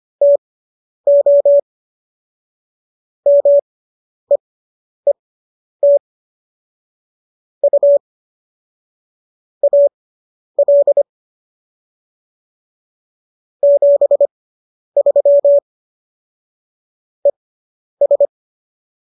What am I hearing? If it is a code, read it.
Morse code, heard as a keyed tone, about 25 wpm, spaced as for 10 wpm, 570 Hz. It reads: TO MEET U AL 73 ES